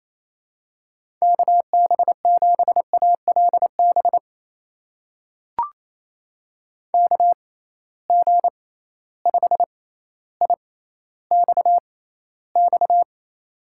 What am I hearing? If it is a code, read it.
Morse code, heard as a keyed tone, about 28 wpm, spaced as for 14 wpm, 700 Hz. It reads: KB7ALB KG5IXX
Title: USA Call Signs - 6x Character Spacing - 28wpm